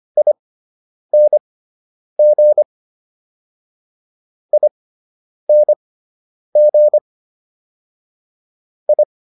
Morse code, heard as a keyed tone, about 25 wpm, spaced as for 9 wpm, 600 Hz. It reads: ING ING I
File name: ING6 09